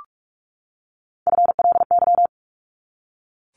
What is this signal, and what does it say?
Morse code, heard as a keyed tone, about 44 wpm, 700 Hz. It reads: FLY
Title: Top 500 Words 45wpm